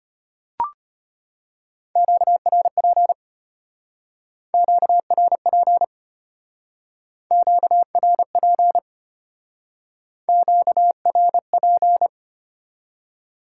Morse code, heard as a keyed tone, about 32 wpm, 700 Hz. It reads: QRP QRP QRP QRP